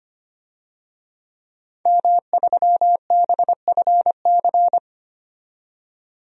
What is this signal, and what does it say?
Morse code, heard as a keyed tone, about 25 wpm, 700 Hz. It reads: M3BFC
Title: DXCC-100 Call Signs - 5 Characters - 25wpm